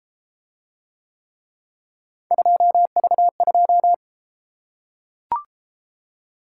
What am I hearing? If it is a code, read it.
Morse code, heard as a keyed tone, about 30 wpm, 700 Hz. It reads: 2V2